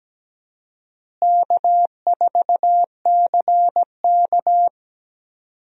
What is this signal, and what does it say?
Morse code, heard as a keyed tone, about 17 wpm, 700 Hz. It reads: K4CK
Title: USA Call Signs - 4 Characters - Repeated 2x - 17wpm